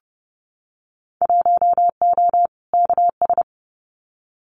Morse code, heard as a keyed tone, about 30 wpm, 700 Hz. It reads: 1O KS
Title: Winter Field Day Exchanges 30wpm